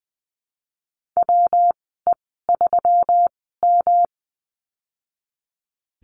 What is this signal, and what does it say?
Morse code, heard as a keyed tone, about 20 wpm, spaced as for 14 wpm, 700 Hz. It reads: WE3M
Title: 122 Lesson 20g - Number 3 - Call Signs 20wpm